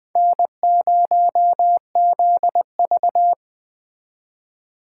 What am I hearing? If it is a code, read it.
Morse code, heard as a keyed tone, about 21 wpm, 700 Hz. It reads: N0ZV
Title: Speed Racing - USA 1x2 Call Signs 17wpm